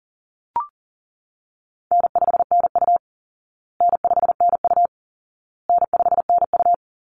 Morse code, heard as a keyed tone, about 40 wpm, 700 Hz. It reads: N5NU N5NU N5NU
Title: Easy Call Signs Repeated 3x - 40wpm